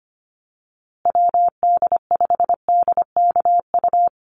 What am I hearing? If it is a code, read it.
Morse code, heard as a keyed tone, about 25 wpm, 700 Hz. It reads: WD5DKU
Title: USA Call Signs Repeated 3x 25wpm